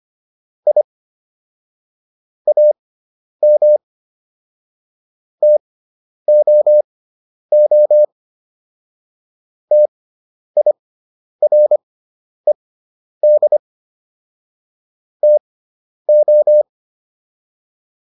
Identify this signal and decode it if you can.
Morse code, heard as a keyed tone, about 25 wpm, spaced as for 10 wpm, 600 Hz. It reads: I AM TOO TIRED TO